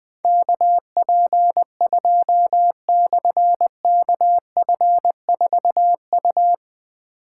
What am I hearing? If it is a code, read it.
Morse code, heard as a keyed tone, about 20 wpm, 700 Hz. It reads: KP2/KF4U
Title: Difficult Call Signs 20wpm